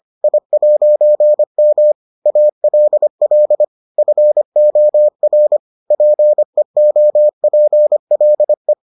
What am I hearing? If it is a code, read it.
Morse code, heard as a keyed tone, about 25 wpm, 600 Hz. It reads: I'M ALL FOR PEOPLE